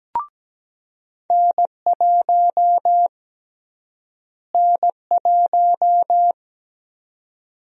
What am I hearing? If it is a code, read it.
Morse code, heard as a keyed tone, about 17 wpm, 700 Hz. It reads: N1 N1
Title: USA Call Sign Prefixes - Repeated 2x - 17wpm